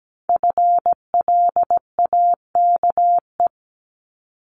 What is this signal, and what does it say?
Morse code, heard as a keyed tone, about 17 wpm, 700 Hz. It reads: FLAKE
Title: Multi-letter Phonemes 17wpm